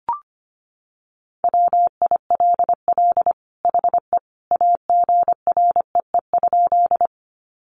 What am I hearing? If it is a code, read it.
Morse code, heard as a keyed tone, about 25 wpm, 700 Hz. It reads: WILL HE AGREE?
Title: Sentences from Top 1100 Words 25wpm